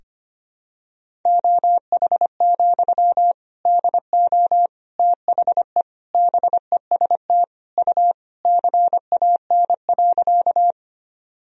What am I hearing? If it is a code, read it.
Morse code, heard as a keyed tone, about 25 wpm, 700 Hz. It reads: OH, DO THE BEST U CAN.
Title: Sentences from Top 500 Words and Abbreviations 25wpm